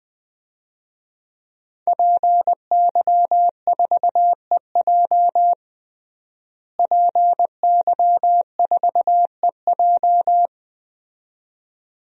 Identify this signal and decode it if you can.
Morse code, heard as a keyed tone, about 21 wpm, 700 Hz. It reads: PY4EJ PY4EJ
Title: DXCC-100 Call Signs - 5 Characters - Repeated 2x - 20wpm